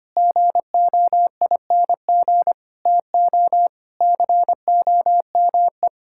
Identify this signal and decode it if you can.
Morse code, heard as a keyed tone, about 25 wpm, 700 Hz. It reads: GOING TO COME
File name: practice / 25wpm / Sentences with 3-Word Phrases - 25wpm